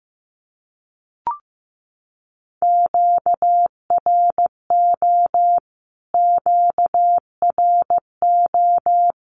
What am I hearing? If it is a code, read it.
Morse code, heard as a keyed tone, about 15 wpm, 700 Hz. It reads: QRO QRO
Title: Q-Codes Repeated 3x 15wpm